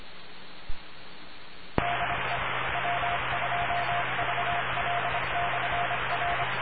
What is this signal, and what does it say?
Morse code, heard as a keyed tone, about 26 wpm, 700 Hz, with noise. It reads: DE W2VL/R